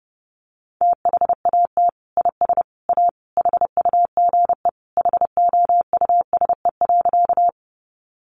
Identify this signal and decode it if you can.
Morse code, heard as a keyed tone, about 30 wpm, 700 Hz. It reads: THAT IS A HUGE HOUSE.
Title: Sentences from Top 1100 Words 30wpm